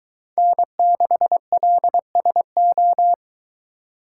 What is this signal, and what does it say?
Morse code, heard as a keyed tone, about 23 wpm, 700 Hz. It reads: N6LSO